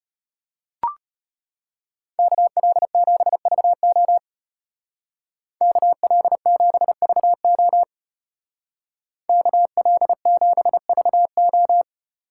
Morse code, heard as a keyed tone, about 33 wpm, 700 Hz. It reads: KL7VO KL7VO KL7VO